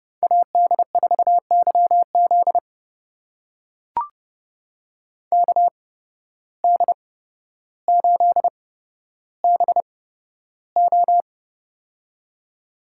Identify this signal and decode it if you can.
Morse code, heard as a keyed tone, about 29 wpm, spaced as for 13 wpm, 700 Hz. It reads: AD4YZ KD8BO
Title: USA Call Signs - 5 Characters - 8x Character Spacing - 30wpm